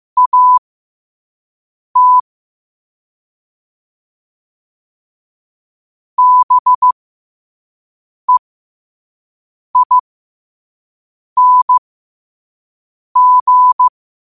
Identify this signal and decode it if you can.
Morse code, heard as a keyed tone, about 15 wpm, spaced as for 5 wpm, 1000 Hz. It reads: AT BEING
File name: qotdfiles / 5wpm